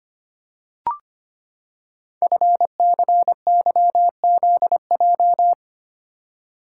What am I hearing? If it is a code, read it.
Morse code, heard as a keyed tone, about 25 wpm, 700 Hz. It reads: FCYZJ
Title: Code Groups - 5 Letters - 25wpm